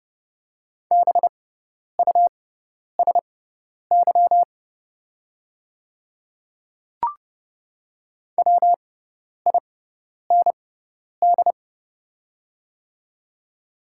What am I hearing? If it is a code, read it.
Morse code, heard as a keyed tone, about 30 wpm, spaced as for 10 wpm, 700 Hz. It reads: BUSY WIND
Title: 6x Character Spacing - Top 500 Words - 30wpm